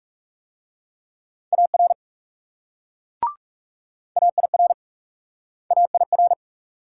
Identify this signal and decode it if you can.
Morse code, heard as a keyed tone, about 39 wpm, 700 Hz. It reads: AR AIR AIR